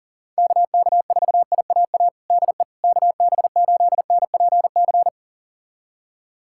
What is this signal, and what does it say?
Morse code, heard as a keyed tone, about 37 wpm, 700 Hz. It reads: KK4IAA DE KB8NPC
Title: USA Callsign DE Callsign - 6 Characters - 40wpm